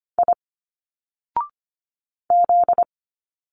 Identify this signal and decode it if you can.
Morse code, heard as a keyed tone, about 25 wpm, 700 Hz. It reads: I Z
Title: Single Letters 25wpm